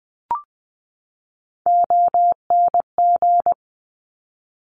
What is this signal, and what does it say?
Morse code, heard as a keyed tone, about 20 wpm, 700 Hz. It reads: ONG
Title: Top Three Letter Combinations in Words 20wpm